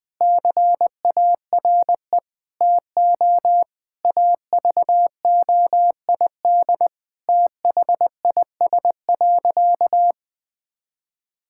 Morse code, heard as a keyed tone, about 20 wpm, 700 Hz. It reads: CARE TO AVOID THIS.
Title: Sentences from Top 1000 Words - Review Part 6 - 20wpm